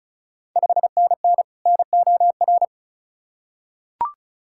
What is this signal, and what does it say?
Morse code, heard as a keyed tone, about 34 wpm, 700 Hz. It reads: 5NN NOR
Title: Ontario QSO Party Exchange 35wpm